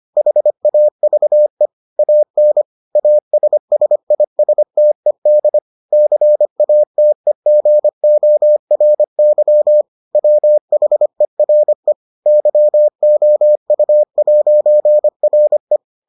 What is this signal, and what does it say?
Morse code, heard as a keyed tone, about 25 wpm, 600 Hz. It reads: HAVE AN ASSISTED CATEGORY WHERE YOU'RE